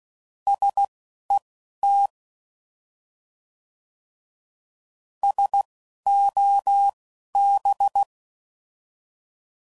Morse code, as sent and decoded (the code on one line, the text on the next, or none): ... / . / - / ... / --- / -...
S E T S O B